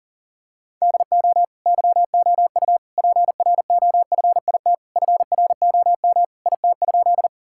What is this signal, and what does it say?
Morse code, heard as a keyed tone, about 40 wpm, 700 Hz. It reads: DO YOU PROFIT FROM IT?
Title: Sentences from Top 1000 Words - Review Part 5 - 40wpm